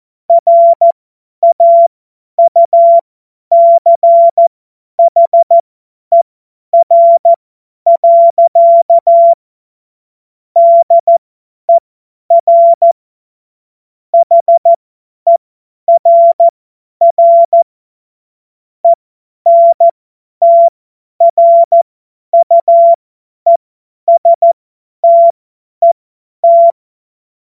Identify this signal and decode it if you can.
Morse code, heard as a keyed tone, about 14 wpm, spaced as for 10 wpm, 680 Hz. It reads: RAUCHER. DER HERR ENTRUESTET
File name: Morsekurs / Witze / Witz 09 April 2024 50 F70